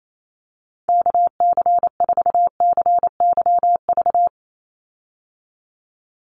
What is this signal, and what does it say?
Morse code, heard as a keyed tone, about 28 wpm, 700 Hz. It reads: KC4CYV
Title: USA Call Signs 28wpm